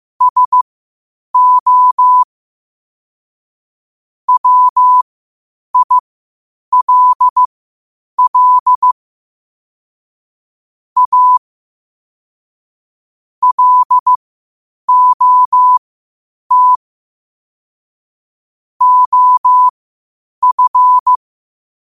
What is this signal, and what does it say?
Morse code, heard as a keyed tone, about 15 wpm, spaced as for 8 wpm, 1000 Hz. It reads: SO WILL A LOT OF
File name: qotdfiles / 8wpm